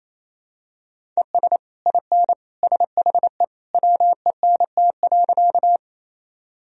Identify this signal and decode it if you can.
Morse code, heard as a keyed tone, about 27 wpm, 700 Hz. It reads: ES IN SHE WENT.
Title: Sentences from Top 200 Words and Abbreviations 28wpm